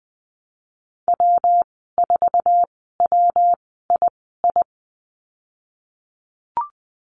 Morse code, heard as a keyed tone, about 20 wpm, spaced as for 15 wpm, 700 Hz. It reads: W4WII
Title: 122 Lesson 20g - Number 3 - Call Signs 20wpm